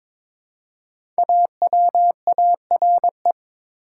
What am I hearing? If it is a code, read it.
Morse code, heard as a keyed tone, about 22 wpm, 700 Hz. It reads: AWARE
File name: 5 Letter Words - First 3 Letters Given - 22wpm